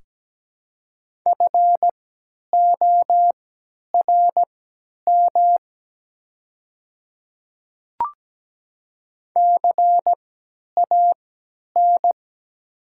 Morse code, heard as a keyed tone, about 17 wpm, spaced as for 10 wpm, 690 Hz. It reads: FORM CAN
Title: 3x Character Spacing - Top 200 Words - 17wpm